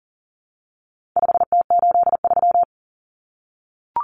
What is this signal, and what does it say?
Morse code, heard as a keyed tone, about 40 wpm, 700 Hz. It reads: 5T83
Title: Common Cut Numbers - 4-Digit 40wpm